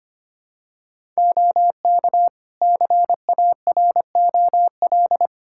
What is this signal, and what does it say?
Morse code, heard as a keyed tone, about 25 wpm, 700 Hz. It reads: OK CAROL
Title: K1USN SST QSO Practice 25wpm